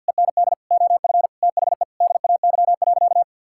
Morse code, heard as a keyed tone, about 50 wpm, 700 Hz. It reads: END OF THE DAY.
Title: Sentences from Top 1000 Words - Review Part 6 - 50wpm